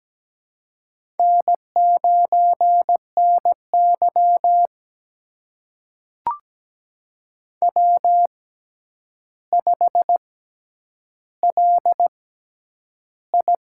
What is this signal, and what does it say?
Morse code, heard as a keyed tone, about 17 wpm, spaced as for 9 wpm, 700 Hz. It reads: N9NY W5LI
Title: USA Call Signs - 6x Character Spacing - 17wpm